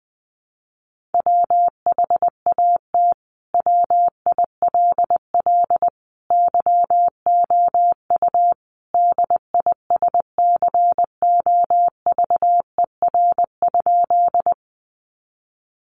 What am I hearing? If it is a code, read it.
Morse code, heard as a keyed tone, about 20 wpm, 700 Hz. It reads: WHAT WILL YOU DISCOVER?
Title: Sentences from Top 1100 Words 20wpm